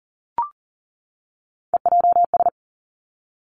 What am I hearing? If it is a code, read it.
Morse code, heard as a keyed tone, about 40 wpm, 700 Hz. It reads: EJS
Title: Three Challenging Letter-Number Combinations 40wpm